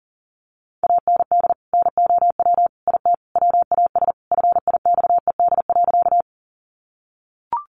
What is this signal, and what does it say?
Morse code, heard as a keyed tone, about 40 wpm, 700 Hz. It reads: AND NOW IT WAS FIXED.
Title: Sentences from Top 1100 Words 40wpm